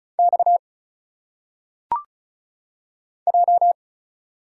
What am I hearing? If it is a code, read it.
Morse code, heard as a keyed tone, about 36 wpm, 700 Hz. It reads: X J
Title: Single Letters - Phonetic Pronunciation 35wpm